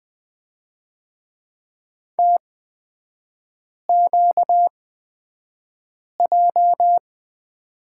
Morse code, heard as a keyed tone, about 20 wpm, spaced as for 5 wpm, 690 Hz. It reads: TQJ
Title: 255 Lesson 41e - Period - Three Characters 20wpm